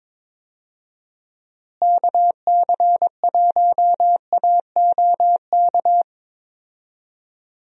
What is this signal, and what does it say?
Morse code, heard as a keyed tone, about 22 wpm, 700 Hz. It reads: KC1AOK
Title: USA Call Signs - 6 Characters - Repeated 2x - 22wpm